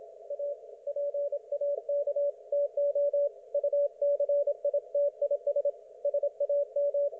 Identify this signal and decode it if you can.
Morse code, heard as a keyed tone, about 26 wpm, 560 Hz. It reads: A PAK TO UCITIS SAM
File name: Archiv / CUC179 / ok1fhi cul179